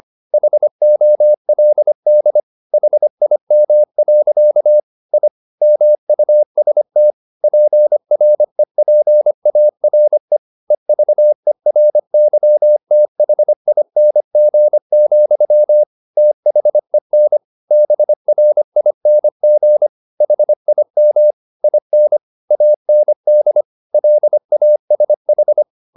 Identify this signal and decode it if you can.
Morse code, heard as a keyed tone, about 25 wpm, 600 Hz. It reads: HOLD HIM. I MUST PREPARE EVERYTHING, THEN BRING HIM IN AND LASH